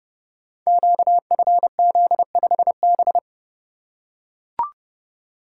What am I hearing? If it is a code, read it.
Morse code, heard as a keyed tone, about 30 wpm, 700 Hz. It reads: QFZ5B
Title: Code Groups - 5 Letter:Numbers - 30wpm